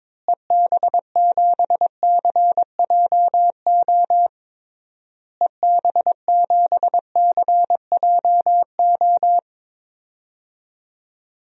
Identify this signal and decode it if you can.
Morse code, heard as a keyed tone, about 22 wpm, 700 Hz. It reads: EB7CJO EB7CJO
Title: DXCC-100 Call Signs - 6 Characters - Repeated 3x - 22wpm